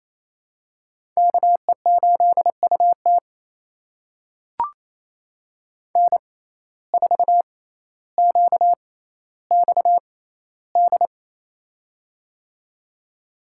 Morse code, heard as a keyed tone, about 28 wpm, spaced as for 14 wpm, 700 Hz. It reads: KE8UT N4QXD
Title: USA Call Signs - 5 Characters - 6x Character Spacing - 28wpm